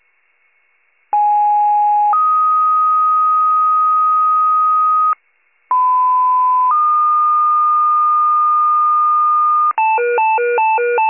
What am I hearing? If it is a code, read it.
Two-tone page, heard as a one-second tone, then a three-second tone, about 830 then 1250 Hz, about 1010 then 1220 Hz. It is heard from 1.1 to 9.7 s.